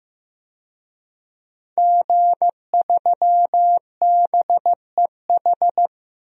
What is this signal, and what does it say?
Morse code, heard as a keyed tone, about 15 wpm, 700 Hz. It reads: G3BEH
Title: DXCC-100 Call Signs - 5 Characters - 15wpm